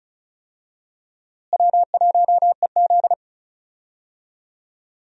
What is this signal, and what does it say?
Morse code, heard as a keyed tone, about 34 wpm, 700 Hz. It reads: W1EZ
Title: USA Call Signs - 4 Characters - Repeated 2x - 35wpm